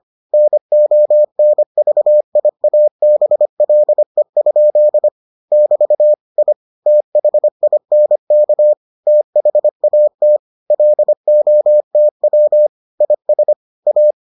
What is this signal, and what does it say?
Morse code, heard as a keyed tone, about 25 wpm, 600 Hz. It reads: NONVIABLE? <BT> I THINK THAT LOTW IS A